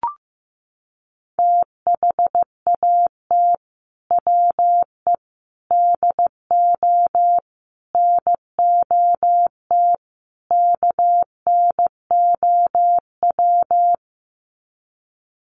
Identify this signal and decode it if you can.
Morse code, heard as a keyed tone, about 15 wpm, 700 Hz. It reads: THAT WE DO NOT KNOW
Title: Sets of 5 Words Encore2 15wpm